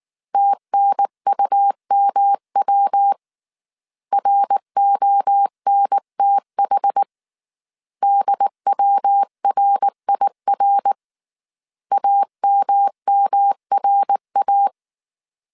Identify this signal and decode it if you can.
Morse code, heard as a keyed tone, about 19 wpm, 800 Hz. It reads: TNUMW RONTH DWRIR AMMRA
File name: CW / DET / 1 3 1500C S B 18wpm 1916